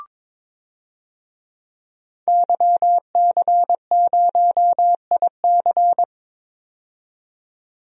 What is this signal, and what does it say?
Morse code, heard as a keyed tone, about 22 wpm, 700 Hz. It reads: YC0IC
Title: DXCC-100 Call Signs - 5 Characters - 22wpm